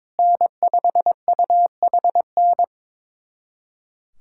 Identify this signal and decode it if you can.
Morse code, heard as a keyed tone, about 22 wpm, 700 Hz. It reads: N5UHN
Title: USA Callsign DE Callsign - 5 Characters - 22wpm